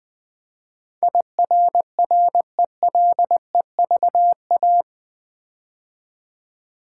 Morse code, heard as a keyed tone, about 20 wpm, 700 Hz. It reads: IRRELEVA
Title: Build A Word - 4 Letters At A Time - 10-Letter Words - 20wpm